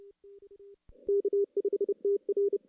Morse code, heard as a keyed tone, about 34 wpm, 400 Hz. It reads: K5TR
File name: contests / CW-SS-k5tr- -1600